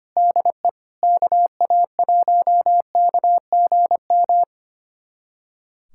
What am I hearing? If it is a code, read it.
Morse code, heard as a keyed tone, about 25 wpm, 700 Hz. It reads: DE KA1KGM
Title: USA Callsign DE Callsign - 6 Characters - 25wpm